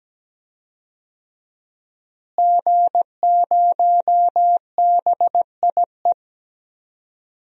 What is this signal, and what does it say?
Morse code, heard as a keyed tone, about 17 wpm, 700 Hz. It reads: G0BIE
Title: DXCC-100 Call Signs - 5 Characters - Repeated 3x - 17wpm